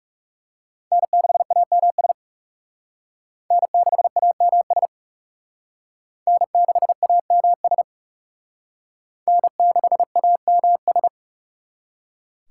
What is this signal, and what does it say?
Morse code, heard as a keyed tone, about 37 wpm, 700 Hz. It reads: N6AMS N6AMS N6AMS N6AMS
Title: Speed Racing - USA 1x3 Call Signs 30wpm